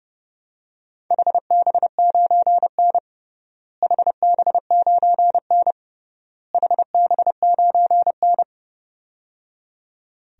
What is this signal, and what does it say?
Morse code, heard as a keyed tone, about 30 wpm, 700 Hz. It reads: HB9N HB9N HB9N
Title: DXCC-100 Call Signs - 4 Characters - Repeated 3x - 30wpm